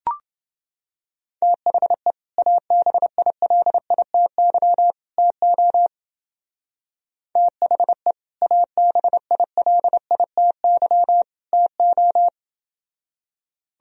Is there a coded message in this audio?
Yes: Morse code from 1.4 to 12.3 s, heard as a keyed tone, about 28 wpm, 700 Hz. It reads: THE ABILITY TO THE ABILITY TO